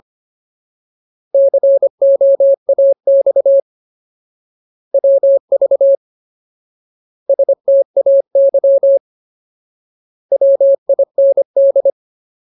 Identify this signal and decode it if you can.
Morse code, heard as a keyed tone, about 25 wpm, 550 Hz. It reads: COAX WV STAY WIND
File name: Ultimate Word List / 25WPM Ultimate